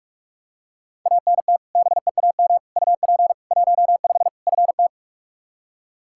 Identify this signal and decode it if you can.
Morse code, heard as a keyed tone, about 46 wpm, 700 Hz. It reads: ANT BEAM UP 15 FT
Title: Practice QSO - Exchange 2 - Info - 45wpm